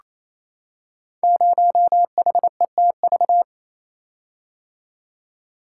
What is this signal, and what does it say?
Morse code, heard as a keyed tone, about 27 wpm, 700 Hz. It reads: 0HETV
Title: Code Groups - 5 Letter:Numbers - 28wpm